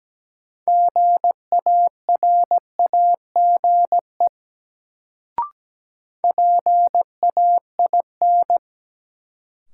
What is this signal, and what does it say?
Morse code, heard as a keyed tone, about 17 wpm, 700 Hz. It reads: GARAGE PAIN
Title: Sentences from Top 1600 Words - New Word Drill - 17wpm